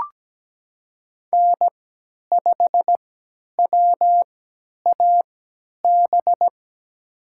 Morse code, heard as a keyed tone, about 17 wpm, spaced as for 10 wpm, 700 Hz. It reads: N5WAB
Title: USA Call Signs - 3x Character Spacing - 17wpm